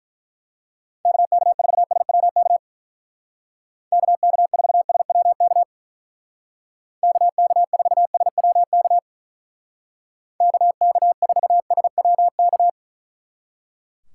Morse code, heard as a keyed tone, about 42 wpm, 700 Hz. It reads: KK4SWK KK4SWK KK4SWK KK4SWK
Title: Speed Racing - USA 2x3 Call Signs 35wpm